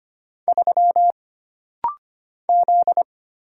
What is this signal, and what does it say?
Morse code, heard as a keyed tone, about 25 wpm, 700 Hz. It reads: 3 Z
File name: Single Character (Letter:Number:Punctuation:Prosign) - Rapid-Fire - 25wpm